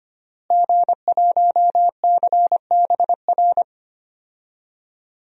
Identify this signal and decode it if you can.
Morse code, heard as a keyed tone, about 25 wpm, 700 Hz. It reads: G1CBR
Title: DXCC-100 Call Signs - 5 Characters - Repeated 3x - 25wpm